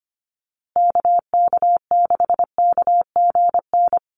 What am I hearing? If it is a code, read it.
Morse code, heard as a keyed tone, about 25 wpm, 700 Hz. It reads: KK6KGN
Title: USA Call Signs Repeated 3x 25wpm